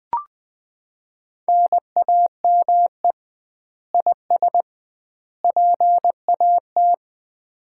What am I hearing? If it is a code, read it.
Morse code, heard as a keyed tone, about 20 wpm, 700 Hz. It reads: NAME IS PAT
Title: Common QSO Phrases - 2x Word-Spacing - 20wpm